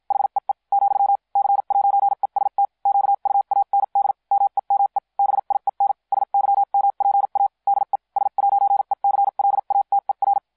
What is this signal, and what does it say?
Morse code, heard as a keyed tone, about 53 wpm, spaced as for 44 wpm, 800 Hz. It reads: HEE , C'EST QUAND MEME BIEN SYMPA DE S'ECLATER